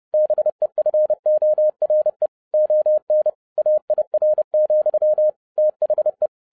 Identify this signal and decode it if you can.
Morse code, heard as a keyed tone, about 30 wpm, 610 Hz. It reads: BEFORE ON AIR, THE